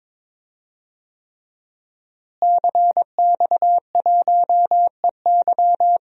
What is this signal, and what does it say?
Morse code, heard as a keyed tone, about 21 wpm, 700 Hz. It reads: CX1EY
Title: DXCC-100 Call Signs - 5 Characters - Repeated 3x - 22wpm